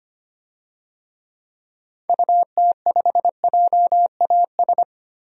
Morse code, heard as a keyed tone, about 25 wpm, 700 Hz. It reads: UT5JAS
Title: DXCC-100 Call Signs - 6 Characters - 25wpm